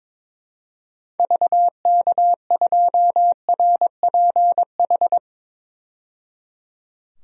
Morse code, heard as a keyed tone, about 22 wpm, 700 Hz. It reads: VK2RPH
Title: DXCC-100 Call Signs - 6 Characters - 22wpm